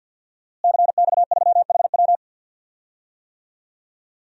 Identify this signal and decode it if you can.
Morse code, heard as a keyed tone, about 52 wpm, 700 Hz. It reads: CX3HW
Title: DXCC-100 Call Signs - 5 Characters - Repeated 3x - 50wpm